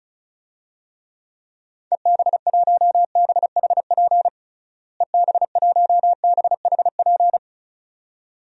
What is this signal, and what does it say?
Morse code, heard as a keyed tone, about 36 wpm, 700 Hz. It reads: EB1BHP EB1BHP
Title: DXCC-100 Call Signs - 6 Characters - Repeated 2x - 35wpm